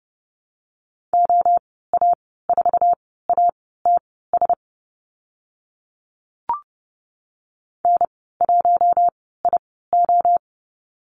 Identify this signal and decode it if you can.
Morse code, heard as a keyed tone, about 30 wpm, spaced as for 17 wpm, 690 Hz. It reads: OA4ATS N1IO
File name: Lesson 15g - Number 5 - Call Signs 30wpm